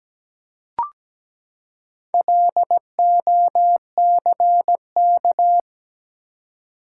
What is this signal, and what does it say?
Morse code, heard as a keyed tone, about 17 wpm, 700 Hz. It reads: LOCK
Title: Four Letter Words - 17wpm